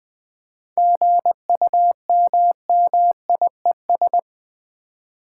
Morse code, heard as a keyed tone, about 20 wpm, 700 Hz. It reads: GUMMIES
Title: Easy Words with Hints - 20wpm